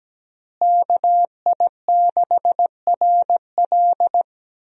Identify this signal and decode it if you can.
Morse code, heard as a keyed tone, about 17 wpm, 700 Hz. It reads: KI6RL
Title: USA Call Signs - 5 Characters - Repeated 2x - 17wpm